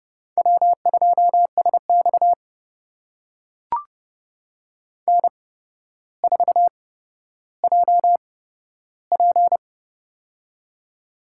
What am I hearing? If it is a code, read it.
Morse code, heard as a keyed tone, about 30 wpm, spaced as for 13 wpm, 700 Hz. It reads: W2SX N4JP
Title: USA Call Signs - 4 Characters - 8x Character Spacing - 30wpm